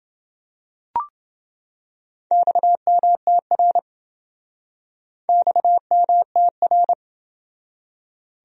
Morse code, heard as a keyed tone, about 27 wpm, 700 Hz. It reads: XMTR XMTR